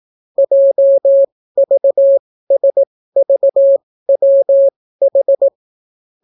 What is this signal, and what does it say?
Morse code, heard as a keyed tone, about 18 wpm, spaced as for 14 wpm, 550 Hz. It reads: JVSVWH